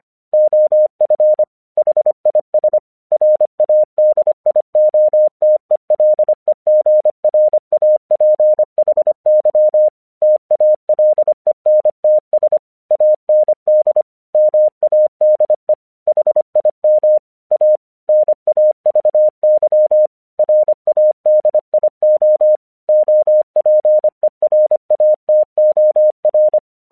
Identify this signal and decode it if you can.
Morse code, heard as a keyed tone, about 25 wpm, 610 Hz. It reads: OF HIS RADIOTELEGRAPHY TALENTS AND MADE HIM A NAVY RADIO OPERATOR
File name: 1 English 25wpm / AQC09